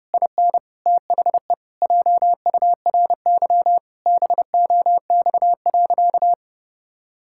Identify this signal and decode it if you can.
Morse code, heard as a keyed tone, about 30 wpm, 700 Hz. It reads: IN THE JURY BOX.
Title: Pangram Challenge v2 - 30wpm